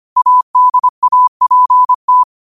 Morse code, heard as a keyed tone, about 25 wpm, 1000 Hz. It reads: ADAPT
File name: qotdfiles / 25wpm